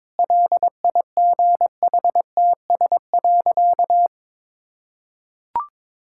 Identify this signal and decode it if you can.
Morse code, heard as a keyed tone, about 22 wpm, 690 Hz. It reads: LIGHTS.
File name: practice / 22wpm / Easy Sentences with Hints - 22wpm